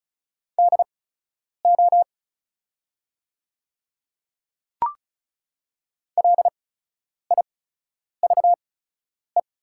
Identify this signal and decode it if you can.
Morse code, heard as a keyed tone, about 36 wpm, spaced as for 10 wpm, 700 Hz. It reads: DO LIVE
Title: 8x Character Spacing - Top 200 Words - 35wpm